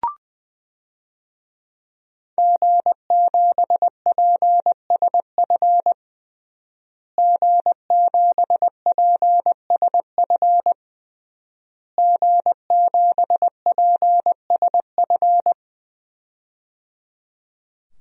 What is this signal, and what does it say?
Morse code, heard as a keyed tone, about 21 wpm, 700 Hz. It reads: G7PSF G7PSF G7PSF
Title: DXCC-100 Call Signs - 5 Characters - Repeated 3x - 20wpm